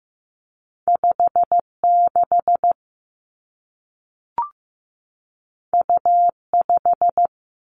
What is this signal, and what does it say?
Morse code, heard as a keyed tone, about 15 wpm, 700 Hz. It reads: 5 6 U 5
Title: Two Challenging Letter-Number Combinations 15wpm